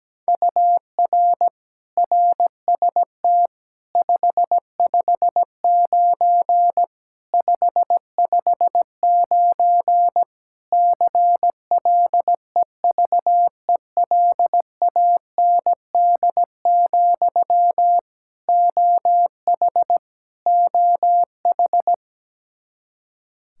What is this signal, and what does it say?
Morse code, heard as a keyed tone, about 17 wpm, 700 Hz. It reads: UR RST 559 559 CLEVELAND, OH OH
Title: Practice QSO - Exchange 1 - Beginning - 17wpm